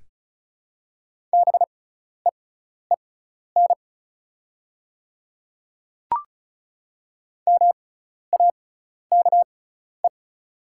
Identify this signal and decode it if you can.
Morse code, heard as a keyed tone, about 36 wpm, spaced as for 12 wpm, 700 Hz. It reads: BEEN MAKE
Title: 6x Character Spacing - Top 100 Words - 35wpm